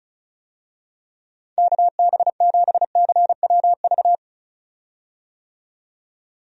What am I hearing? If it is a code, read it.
Morse code, heard as a keyed tone, about 35 wpm, 700 Hz. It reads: KB7CWV
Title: USA Call Signs - 6 Characters - 35wpm